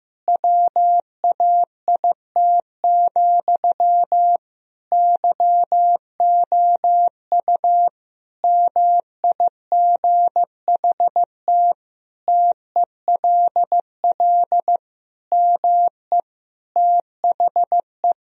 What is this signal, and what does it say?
Morse code, heard as a keyed tone, about 15 wpm, 700 Hz. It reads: WAIT, YOU MIGHT TELL ME THE
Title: Sentences from Top 1000 Words - Review Part 1 - 15wpm